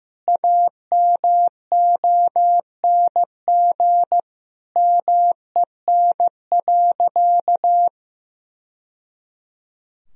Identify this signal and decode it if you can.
Morse code, heard as a keyed tone, about 16 wpm, 700 Hz. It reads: AMONG MEN.
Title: Sentences from Top 1000 Words - Review Part 3 - 15wpm